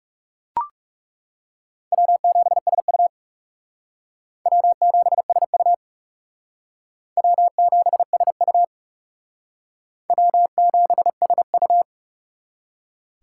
Morse code, heard as a keyed tone, about 38 wpm, 700 Hz. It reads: W7SU W7SU W7SU W7SU